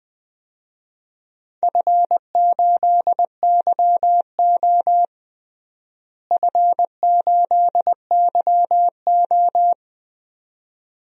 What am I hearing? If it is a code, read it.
Morse code, heard as a keyed tone, about 21 wpm, 700 Hz. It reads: F8YO F8YO
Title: DXCC-100 Call Signs - 4 Characters - Repeated 2x - 20wpm